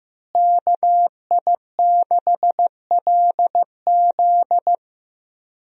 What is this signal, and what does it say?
Morse code, heard as a keyed tone, about 15 wpm, 700 Hz. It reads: KI6LZ
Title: USA Call Signs - 5 Characters - Repeated 3x - 15wpm